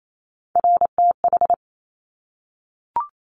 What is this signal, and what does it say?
Morse code, heard as a keyed tone, about 28 wpm, 690 Hz. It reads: RTH
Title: Top Three Letter Combinations in Words 28wpm